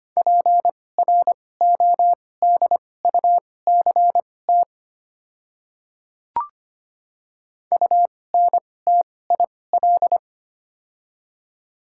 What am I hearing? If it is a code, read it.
Morse code, heard as a keyed tone, about 25 wpm, spaced as for 18 wpm, 700 Hz. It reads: PRODUCT UNTIL
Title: 2x Character Spacing - Top 500 Words - 25wpm